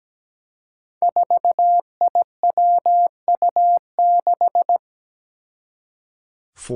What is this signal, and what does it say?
Morse code, heard as a keyed tone, about 17 wpm, 700 Hz. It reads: 4IWU6
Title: Five Challenging Letter-Number Combinations 17wpm